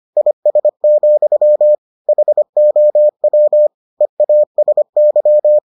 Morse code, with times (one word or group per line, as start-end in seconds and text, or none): 0.2-1.7 IS,
2.1-3.7 HOW
4.0-5.6 EASY